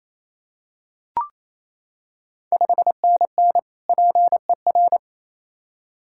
Morse code, heard as a keyed tone, about 28 wpm, 700 Hz. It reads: 5NN PER
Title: Ontario QSO Party Exchange 28wpm